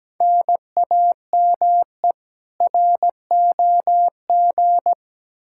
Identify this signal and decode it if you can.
Morse code, heard as a keyed tone, about 17 wpm, 700 Hz. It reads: NAME ROG
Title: Practice QSO - Exchange 1 - Beginning - 17wpm